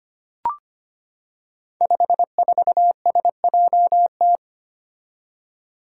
Morse code, heard as a keyed tone, about 25 wpm, 700 Hz. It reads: 54SJT
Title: Five Challenging Letter-Number Combinations 25wpm